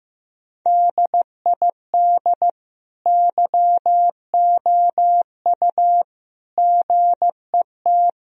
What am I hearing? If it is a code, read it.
Morse code, heard as a keyed tone, about 15 wpm, 700 Hz. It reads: DID YOU GET
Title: Sentences from Top 1000 Words - Review Part 3 - 15wpm